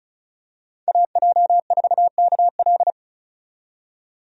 Morse code, heard as a keyed tone, about 36 wpm, 700 Hz. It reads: AJ4KL
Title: USA Call Signs - 5 Characters - 35wpm